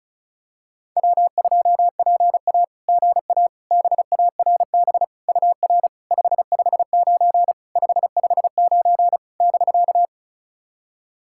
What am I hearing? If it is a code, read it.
Morse code, heard as a keyed tone, about 36 wpm, 690 Hz. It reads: W2PA GA BARB UR 559 559 <BK>
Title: SOTA QSO Practice 35wpm